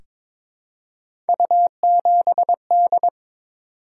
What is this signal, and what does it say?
Morse code, heard as a keyed tone, about 22 wpm, 700 Hz. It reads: U7D
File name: practice / 22wpm / Three Letter-Number Combinations - 22wpm